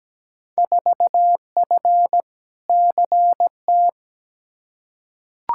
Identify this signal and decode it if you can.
Morse code, heard as a keyed tone, about 17 wpm, 700 Hz. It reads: 4F CT
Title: Field Day Exchanges Encore - 17wpm